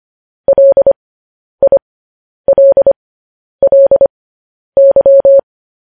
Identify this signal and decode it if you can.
Morse code, heard as a keyed tone, about 25 wpm, spaced as for 10 wpm, 550 Hz. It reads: LILLY
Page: news headlines in morse code at 10 wpm